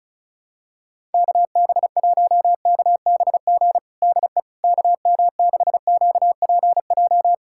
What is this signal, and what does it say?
Morse code, heard as a keyed tone, about 34 wpm, 700 Hz. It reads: KB1KBG DE KM6QPJ
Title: USA Callsign DE Callsign 35wpm